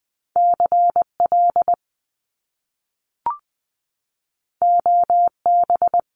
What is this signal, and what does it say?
Morse code, heard as a keyed tone, about 20 wpm, 700 Hz. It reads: CL OB
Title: Top Two Letter Combinations in Words 20wpm